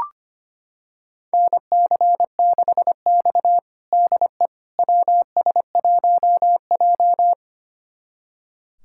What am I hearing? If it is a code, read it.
Morse code, heard as a keyed tone, about 25 wpm, 700 Hz. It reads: NC6X DE WS1J
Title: USA Callsign DE Callsign - 4 Characters - 25wpm